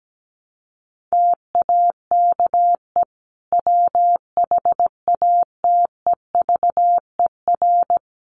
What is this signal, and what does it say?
Morse code, heard as a keyed tone, about 17 wpm, 700 Hz. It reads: TAKE WHATEVER
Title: Sentences from Top 1100 Words 17wpm